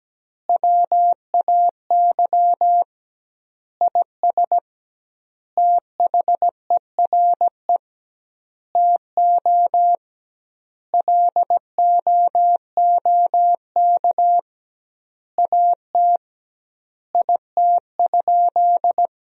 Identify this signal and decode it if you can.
Morse code, heard as a keyed tone, about 17 wpm, 700 Hz. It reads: WAY IS THERE TO LOOK AT IT?
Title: 2x Word-Spacing - Sentences from Top 100 Words - 17wpm